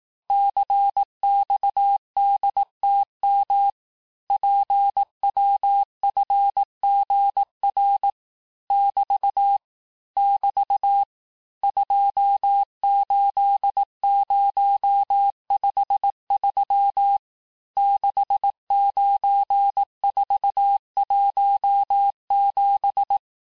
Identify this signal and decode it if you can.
Morse code, heard as a keyed tone, about 18 wpm, 800 Hz. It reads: CXDTM PWFGR <BT> <BT> 28053 69417
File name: cours-du-lundi files / cours-1588-texte-chiffre-vitesse-17MPM